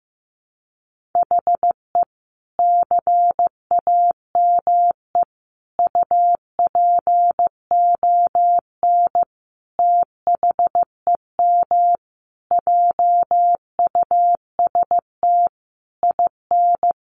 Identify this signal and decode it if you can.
Morse code, heard as a keyed tone, about 15 wpm, 700 Hz. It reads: HE CAME UPON THEM JUST IN
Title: Sentences from Top 1100 Words 15wpm